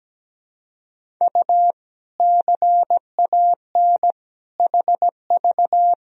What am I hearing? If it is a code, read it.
Morse code, heard as a keyed tone, about 17 wpm, 700 Hz. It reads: U CAN HV
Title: Sentences from Top 100 Words and Abbreviations 17wpm